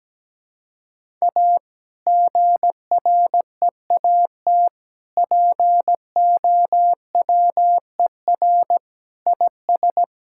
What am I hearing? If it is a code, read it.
Morse code, heard as a keyed tone, about 17 wpm, 700 Hz. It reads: A GREAT POWER IS
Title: Sentences from Top 1000 Words - Review Part 5 - 17wpm